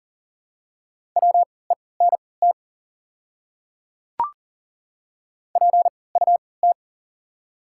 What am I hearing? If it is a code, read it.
Morse code, heard as a keyed tone, about 40 wpm, spaced as for 23 wpm, 700 Hz. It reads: WENT PUT
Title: 3x Character Spacing - Top 200 Words - 40wpm